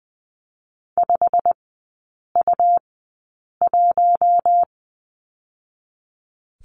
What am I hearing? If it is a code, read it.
Morse code, heard as a keyed tone, about 20 wpm, 700 Hz. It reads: 5 U 1
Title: 107 Lesson 18e - Letter m - Three Characters 20wpm